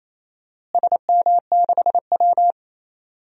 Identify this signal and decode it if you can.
Morse code, heard as a keyed tone, about 27 wpm, 700 Hz. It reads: SM6W
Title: DXCC-100 Call Signs - 4 Characters - Repeated 3x - 28wpm